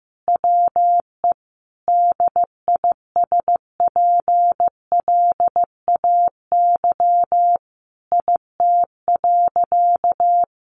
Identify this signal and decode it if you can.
Morse code, heard as a keyed tone, about 15 wpm, 700 Hz. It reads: WE DISPLAY IT.
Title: Sentences from Top 1200 Words 15wpm